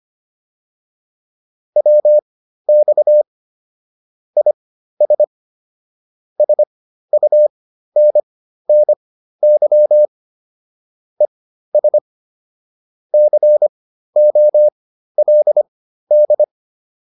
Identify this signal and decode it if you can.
Morse code, heard as a keyed tone, about 25 wpm, spaced as for 13 wpm, 600 Hz. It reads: WX IS SUNNY ES COLD
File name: QSO310 13